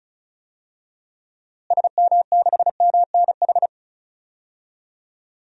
Morse code, heard as a keyed tone, about 34 wpm, 700 Hz. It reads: SM6MNH
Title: DXCC-100 Call Signs - 6 Characters - Repeated 2x - 35wpm